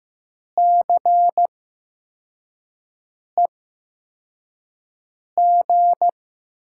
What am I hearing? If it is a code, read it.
Morse code, heard as a keyed tone, about 15 wpm, spaced as for 4 wpm, 700 Hz. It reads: CEG
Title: USA Call Signs - 8x Character Spacing - 15wpm